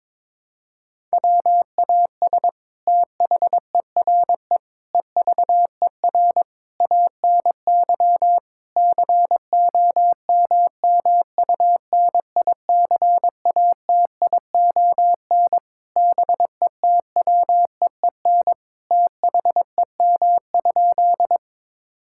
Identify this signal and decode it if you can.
Morse code, heard as a keyed tone, about 22 wpm, 700 Hz. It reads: WAS THERE EVER ANY COMMUNICATION BETWEEN THEM?
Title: Sentences from Top 1000 Words - Review Part 6 - 22wpm